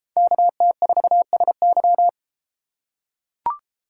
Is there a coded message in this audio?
Yes: Morse code from 0.2 to 2.1 s, heard as a keyed tone, about 32 wpm, 700 Hz. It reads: KT4SY